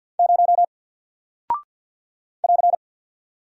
Morse code, heard as a keyed tone, about 48 wpm, 700 Hz. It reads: 0 <AR>
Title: Single Character (Letter:Number:Punctuation:Prosign) Alt 50wpm